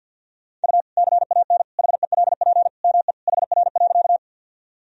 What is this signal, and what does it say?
Morse code, heard as a keyed tone, about 49 wpm, 700 Hz. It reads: U CAN HELP ME HR.
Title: Sentences from Top 200 Words and Abbreviations 50wpm